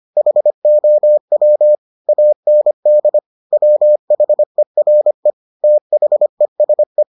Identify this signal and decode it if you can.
Morse code, heard as a keyed tone, about 25 wpm, 600 Hz. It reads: HOW AND WHERE THESE